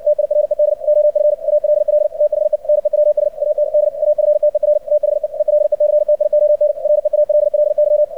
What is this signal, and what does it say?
Morse code, heard as a keyed tone, about 56 wpm, 600 Hz, with radio noise. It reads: TIRED OF ALL THE NEWS AT D AFTER THE ELECTION REALLY
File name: archived / CRSnet